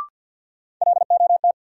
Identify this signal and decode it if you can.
Morse code, heard as a keyed tone, about 53 wpm, 700 Hz. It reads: LOT
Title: Sentences from Top 500 Words - New Word Drill - 50wpm